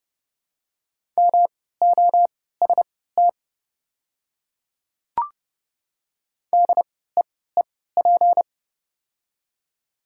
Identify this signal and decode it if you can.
Morse code, heard as a keyed tone, about 30 wpm, spaced as for 17 wpm, 700 Hz. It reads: MOST DEEP
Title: 3x Character Spacing - Top 500 Words - 30wpm